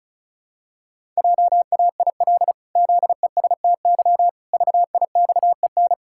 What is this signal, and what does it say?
Morse code, heard as a keyed tone, about 34 wpm, 700 Hz. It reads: JAIL ZESTY VIXEN
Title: Pangram Challenge v2 - 35wpm